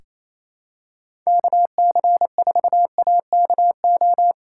Morse code, heard as a keyed tone, about 28 wpm, 700 Hz. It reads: KC4AKO
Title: USA Call Signs - 6 Characters - 2x Character Spacing - 28wpm